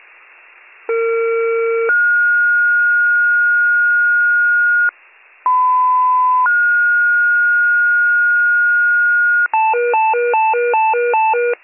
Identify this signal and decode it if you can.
Two-tone page, heard as a one-second tone, then a three-second tone, about 460 then 1430 Hz, about 1010 then 1430 Hz. It is heard from 0.9 to 9.5 s.